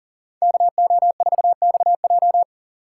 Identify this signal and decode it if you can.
Morse code, heard as a keyed tone, about 49 wpm, 700 Hz. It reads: KO4XJ
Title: Speed Racing - USA 2x2 Call Signs 40wpm